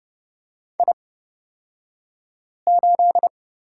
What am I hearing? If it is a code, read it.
Morse code, heard as a keyed tone, about 30 wpm, spaced as for 5 wpm, 700 Hz. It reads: I8
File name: Lesson 37d - Letter x - Two Characters 30wpm